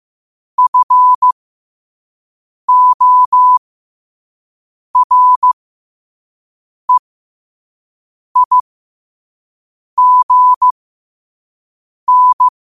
Morse code, heard as a keyed tone, about 15 wpm, spaced as for 5 wpm, 1000 Hz. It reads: FOREIGN
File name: qotdfiles / 5wpm